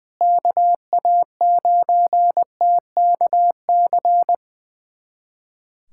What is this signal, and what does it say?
Morse code, heard as a keyed tone, about 20 wpm, 700 Hz. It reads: KA9TKC
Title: USA Callsign DE Callsign 20wpm